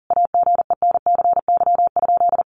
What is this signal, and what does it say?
Morse code, heard as a keyed tone, about 37 wpm, 700 Hz. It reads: AGENCY?
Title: Sentences from Top 1300 Words 40wpm